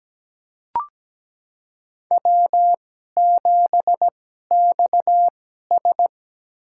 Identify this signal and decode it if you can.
Morse code, heard as a keyed tone, about 17 wpm, spaced as for 12 wpm, 700 Hz. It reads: W7XS
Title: USA Call Signs - 4 Characters - 2x Character Spacing - 17wpm